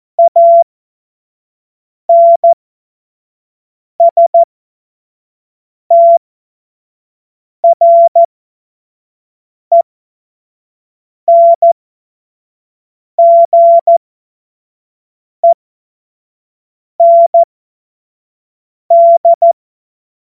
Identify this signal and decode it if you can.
Morse code, heard as a keyed tone, about 14 wpm, spaced as for 5 wpm, 680 Hz. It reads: ANSTRENGEND
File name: Morsekurs / Witze / Witz 27. Dezember 2022 25 F70